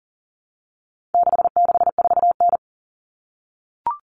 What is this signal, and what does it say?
Morse code, heard as a keyed tone, about 40 wpm, 700 Hz. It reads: 664N
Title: Common Cut Numbers - 4-Digit 40wpm